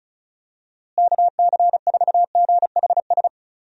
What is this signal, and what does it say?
Morse code, heard as a keyed tone, about 34 wpm, 700 Hz. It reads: KC4GHS
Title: USA Call Signs - 6 Characters - Repeated 2x - 35wpm